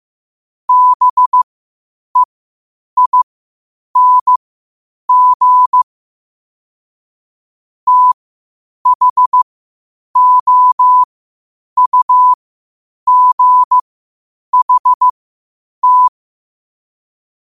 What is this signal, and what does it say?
Morse code, heard as a keyed tone, about 15 wpm, spaced as for 8 wpm, 1000 Hz. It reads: BEING THOUGHT
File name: qotdfiles / 8wpm